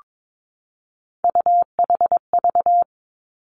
Morse code, heard as a keyed tone, about 22 wpm, 700 Hz. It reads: UHV
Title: Three Challenging Letter-Number Combinations 22wpm